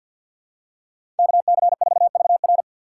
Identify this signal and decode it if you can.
Morse code, heard as a keyed tone, about 53 wpm, 700 Hz. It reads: KC4VR
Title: USA Call Signs - 6x Character Spacing - 50wpm